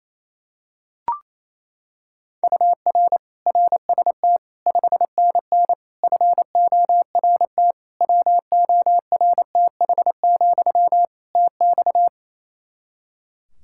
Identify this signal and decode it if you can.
Morse code, heard as a keyed tone, about 28 wpm, 700 Hz. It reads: UR RST 5NN FORT WORTH, TX
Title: Practice QSO - Exchange 1 - Beginning - 28wpm